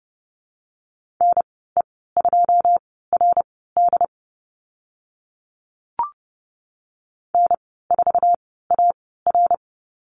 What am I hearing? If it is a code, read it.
Morse code, heard as a keyed tone, about 30 wpm, spaced as for 17 wpm, 700 Hz. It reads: NE2RD N4AR
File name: Lesson 14g - Number 2 - Call Signs 30wpm